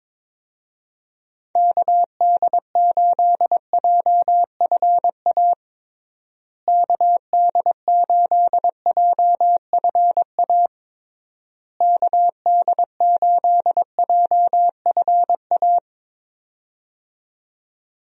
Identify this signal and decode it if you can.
Morse code, heard as a keyed tone, about 22 wpm, 700 Hz. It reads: KD8JFA KD8JFA KD8JFA
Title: USA Call Signs - 6 Characters - Repeated 3x - 22wpm